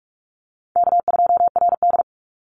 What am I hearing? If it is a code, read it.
Morse code, heard as a keyed tone, about 44 wpm, 700 Hz. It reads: K2RD
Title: Easy Call Signs Repeated 2x - 45wpm